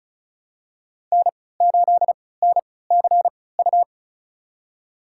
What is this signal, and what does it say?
Morse code, heard as a keyed tone, about 36 wpm, spaced as for 20 wpm, 700 Hz. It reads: N8NCU
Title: USA Call Signs - 3x Character Spacing - 35wpm